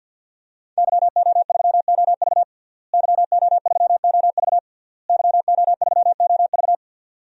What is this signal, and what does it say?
Morse code, heard as a keyed tone, about 52 wpm, 700 Hz. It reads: YO3OV YO3OV YO3OV
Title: DXCC-100 Call Signs - 5 Characters - Repeated 3x - 50wpm